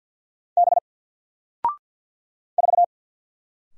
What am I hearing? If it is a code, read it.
Morse code, heard as a keyed tone, about 48 wpm, 700 Hz. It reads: B 4
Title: Single Letter-Number Alt 50wpm